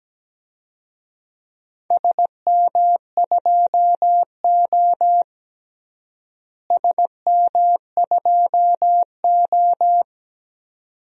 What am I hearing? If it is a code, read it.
Morse code, heard as a keyed tone, about 17 wpm, 700 Hz. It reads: SM2O SM2O
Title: DXCC-100 Call Signs - 4 Characters - Repeated 2x - 17wpm